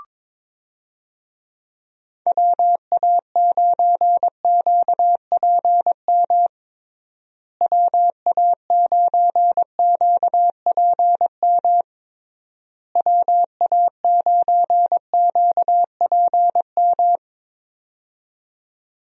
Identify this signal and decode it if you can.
Morse code, heard as a keyed tone, about 22 wpm, 700 Hz. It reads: WA9QPM WA9QPM WA9QPM
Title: USA Call Signs - 6 Characters - Repeated 3x - 22wpm